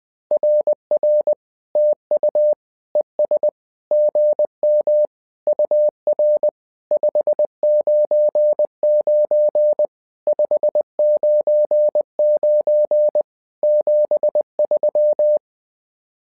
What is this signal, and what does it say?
Morse code, heard as a keyed tone, about 20 wpm, 600 Hz. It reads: RR TU ES GM UR 599 599 73
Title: SOTA QSO Practice 20wpm